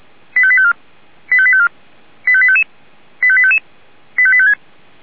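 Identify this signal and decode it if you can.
Five-tone selective call (ZVEI), heard as quick runs of five tones, tones of about 70 ms. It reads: 85854 85854 85855 85855 85856
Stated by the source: Five-tone ZVEI selective call received on an SDR.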